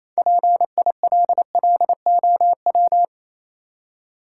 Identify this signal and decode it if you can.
Morse code, heard as a keyed tone, about 28 wpm, 690 Hz. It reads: PILLOW